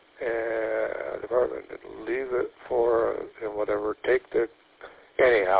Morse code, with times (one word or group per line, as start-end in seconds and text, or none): none